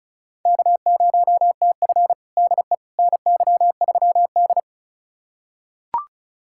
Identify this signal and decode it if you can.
Morse code, heard as a keyed tone, about 36 wpm, 700 Hz. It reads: K0TF DE NY3D
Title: USA Callsign DE Callsign - 4 Characters - 35wpm